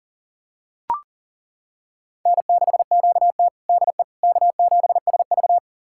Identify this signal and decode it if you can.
Morse code, heard as a keyed tone, about 40 wpm, 700 Hz. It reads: N6QT DE K7SV
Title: USA Callsign DE Callsign - 4 Characters - 40wpm